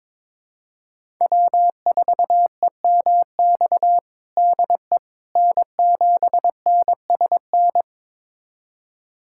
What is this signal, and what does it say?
Morse code, heard as a keyed tone, about 22 wpm, 700 Hz. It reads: W4EMX DE N7NSN
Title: USA Callsign DE Callsign - 5 Characters - 22wpm